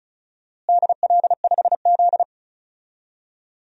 Morse code, heard as a keyed tone, about 34 wpm, 700 Hz. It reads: DL5Z
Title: DXCC-100 Call Signs - 4 Characters - Repeated 2x - 35wpm